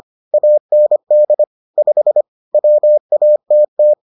Morse code, heard as a keyed tone, about 25 wpm, 600 Hz. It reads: AND 5 WATT